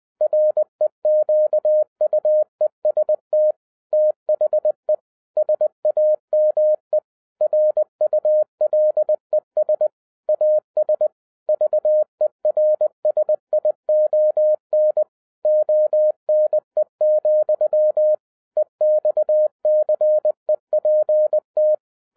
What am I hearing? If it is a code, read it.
Morse code, heard as a keyed tone, about 20 wpm, 610 Hz. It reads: REQUEST THE SAME RULES AS VERSION ONE, EXCEPT